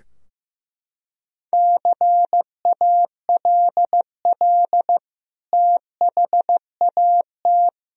Morse code, heard as a keyed tone, about 15 wpm, 700 Hz. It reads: CALL THAT
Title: Sentences from Top 1000 Words - Review Part 4 - 15wpm